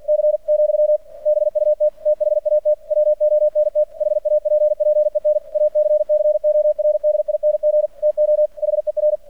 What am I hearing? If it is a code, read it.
Morse code, heard as a keyed tone, about 46 wpm, 600 Hz. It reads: C , BUT THAT WONT HAPPEN ACCORDING TO HER
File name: archived / CRSnet